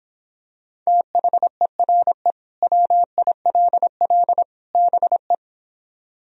Morse code, heard as a keyed tone, about 26 wpm, 700 Hz. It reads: THERE WILL BE